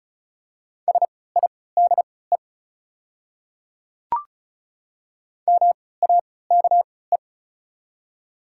Morse code, heard as a keyed tone, about 36 wpm, spaced as for 20 wpm, 700 Hz. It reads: SIDE MAKE
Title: 3x Character Spacing - Top 200 Words - 35wpm